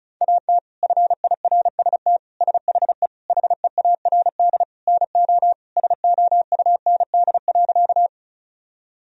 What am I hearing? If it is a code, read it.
Morse code, heard as a keyed tone, about 34 wpm, 700 Hz. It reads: AT FIRST SHE HEARD NO SOUND.
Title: Sentences from Top 1000 Words - Review Part 1 - 35wpm